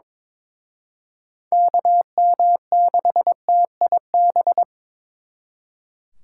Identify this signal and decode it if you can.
Morse code, heard as a keyed tone, about 26 wpm, 700 Hz. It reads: KM6TIB
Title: Speed Racing - USA 2x3 Call Signs 22wpm